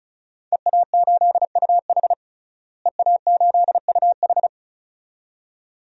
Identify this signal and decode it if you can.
Morse code, heard as a keyed tone, about 34 wpm, 700 Hz. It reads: EA8UH EA8UH
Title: DXCC-100 Call Signs - 5 Characters - Repeated 2x - 35wpm